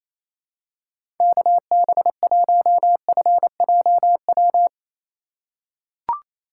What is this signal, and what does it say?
Morse code, heard as a keyed tone, about 28 wpm, 700 Hz. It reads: KB1FJW
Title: USA Call Signs - 3x Character Spacing - 28wpm